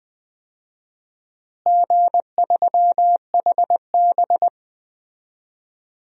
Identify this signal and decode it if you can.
Morse code, heard as a keyed tone, about 20 wpm, 700 Hz. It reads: G3HB
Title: DXCC-100 Call Signs - 4 Characters - Repeated 2x - 20wpm